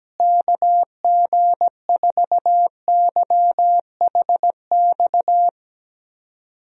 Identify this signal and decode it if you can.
Morse code, heard as a keyed tone, about 17 wpm, 700 Hz. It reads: KG4YHX
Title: USA Call Signs - 6 Characters - Repeated 3x - 17wpm